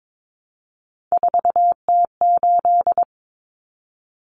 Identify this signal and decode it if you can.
Morse code, heard as a keyed tone, about 22 wpm, 700 Hz. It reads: 4T8
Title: Common Cut Numbers - 3-Digit 22wpm